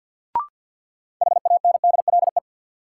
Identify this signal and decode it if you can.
Morse code, heard as a keyed tone, about 53 wpm, 700 Hz. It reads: HANDLE
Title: Sentences from Top 1100 Words - New Word Drill Alt - 50wpm